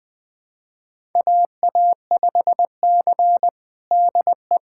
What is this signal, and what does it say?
Morse code, heard as a keyed tone, about 20 wpm, 700 Hz. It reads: AA5C DE
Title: USA Callsign DE Callsign - 4 Characters - 20wpm